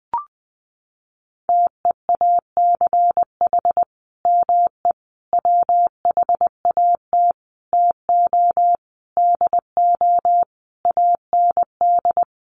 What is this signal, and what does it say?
Morse code, heard as a keyed tone, about 20 wpm, 700 Hz. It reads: TEACH ME WHAT TO DO AND
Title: Sentences from Top 500 Words 20wpm